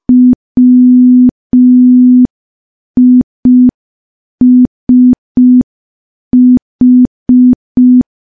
Morse code, heard as a keyed tone, about 5 wpm, 250 Hz. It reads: WISH
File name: morse / morse-training-05WPM